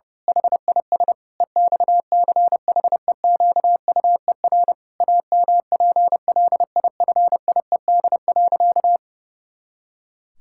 Morse code, heard as a keyed tone, about 32 wpm, 700 Hz. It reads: HIS EXCHEQUER AMPLIFIED.
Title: Pangram Challenge v2 - 30wpm